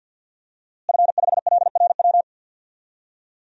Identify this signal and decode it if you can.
Morse code, heard as a keyed tone, about 52 wpm, 700 Hz. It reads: F5LRW
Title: DXCC-100 Call Signs - 5 Characters - Repeated 2x - 50wpm